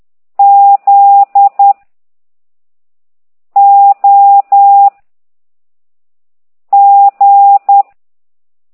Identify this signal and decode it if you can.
Morse code, heard as a keyed tone, about 10 wpm, 800 Hz. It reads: Z O G